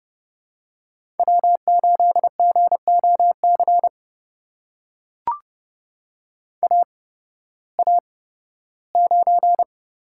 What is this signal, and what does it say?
Morse code, heard as a keyed tone, about 30 wpm, spaced as for 16 wpm, 700 Hz. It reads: W8GOC AA9
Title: USA Call Signs - 5 Characters - 8x Character Spacing - 30wpm